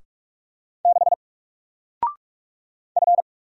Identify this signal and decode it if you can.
Morse code, heard as a keyed tone, about 45 wpm, 700 Hz. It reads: 6 F
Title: Single Letter-Number Alt 45wpm